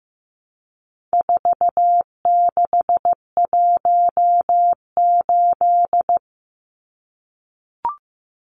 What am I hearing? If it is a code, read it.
Morse code, heard as a keyed tone, about 15 wpm, 700 Hz. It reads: 4618
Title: Numbers - 4-Digit 15wpm